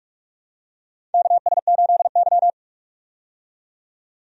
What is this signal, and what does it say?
Morse code, heard as a keyed tone, about 45 wpm, 700 Hz. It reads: KS8Y
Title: USA Call Signs - 4 Characters - 45wpm